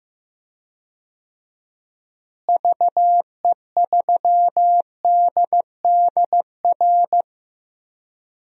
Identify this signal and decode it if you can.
Morse code, heard as a keyed tone, about 15 wpm, 700 Hz. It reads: VE3DDR
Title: DXCC-100 Call Signs - 6 Characters - Repeated 3x - 15wpm